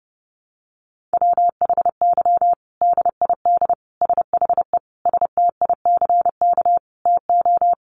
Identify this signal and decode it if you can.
Morse code, heard as a keyed tone, about 29 wpm, 700 Hz. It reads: WHY DID SHE STICK TO
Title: Sentences from Top 1200 Words 30wpm